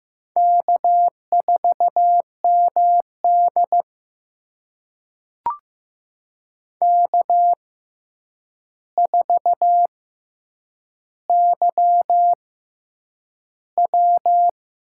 Morse code, heard as a keyed tone, about 15 wpm, spaced as for 8 wpm, 700 Hz. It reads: K4MD K4YW
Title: USA Call Signs - 4 Characters - 6x Character Spacing - 15wpm